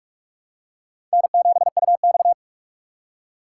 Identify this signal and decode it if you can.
Morse code, heard as a keyed tone, about 41 wpm, 690 Hz. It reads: N7UX
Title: USA Call Signs - 4 Characters - 3x Character Spacing - 45wpm